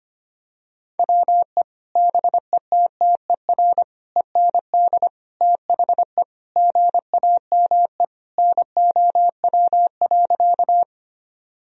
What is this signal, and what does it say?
Morse code, heard as a keyed tone, about 25 wpm, 700 Hz. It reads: WE BETTER END THE GAME NOW.
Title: Sentences from Top 1000 Words - Review Part 3 - 25wpm